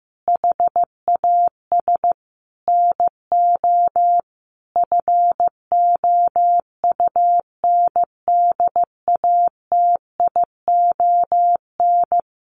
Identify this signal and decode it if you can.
Morse code, heard as a keyed tone, about 15 wpm, 700 Hz. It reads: HAS NO FOUNDATION
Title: Sentences from Top 1200 Words 15wpm